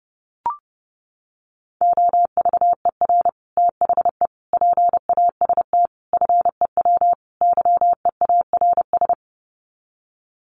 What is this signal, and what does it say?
Morse code, heard as a keyed tone, about 30 wpm, 700 Hz. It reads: OVER THE PAST FEW YEARS
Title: Sets of 5 Words Encore 30wpm